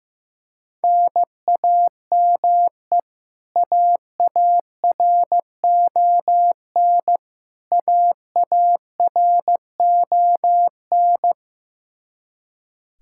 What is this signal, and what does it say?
Morse code, heard as a keyed tone, about 15 wpm, 700 Hz. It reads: NAME AARON AARON
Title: Practice QSO - Exchange 1 - Beginning - 15wpm